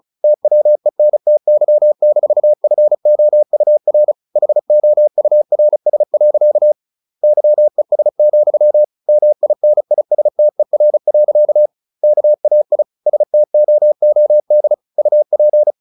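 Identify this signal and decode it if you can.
Morse code, heard as a keyed tone, about 36 wpm, 600 Hz. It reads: TWENTY-FOUR HOURS. YES, MINISTER. KAI STOOD UP
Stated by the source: Morse practice at 35 wpm from CW med Gnister, sending Ken Follett's novel Never in English.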